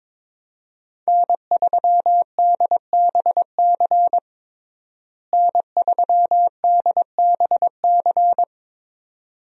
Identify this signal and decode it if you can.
Morse code, heard as a keyed tone, about 22 wpm, 700 Hz. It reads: N3DBC N3DBC
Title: USA Call Signs - 5 Characters - Repeated 2x - 22wpm